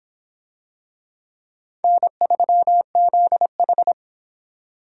Morse code, heard as a keyed tone, about 25 wpm, 700 Hz. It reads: N3ZH